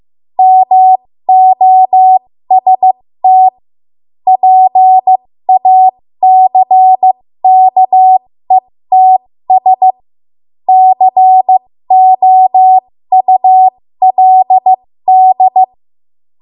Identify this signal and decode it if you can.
Morse code, heard as a keyed tone, about 15 wpm, spaced as for 13 wpm, 750 Hz. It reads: MOST PACKETS COULD
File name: Morse / 13WPM